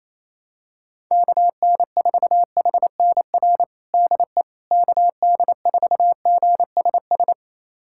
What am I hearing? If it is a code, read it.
Morse code, heard as a keyed tone, about 28 wpm, 700 Hz. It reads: KN4HNR DE KD4GSS
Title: USA Callsign DE Callsign - 6 Characters - 28wpm